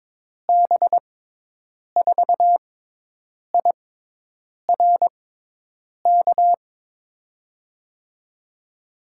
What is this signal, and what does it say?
Morse code, heard as a keyed tone, about 22 wpm, spaced as for 8 wpm, 700 Hz. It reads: B4IRK
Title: USA Call Signs - 6 Characters - 6x Character Spacing - 22wpm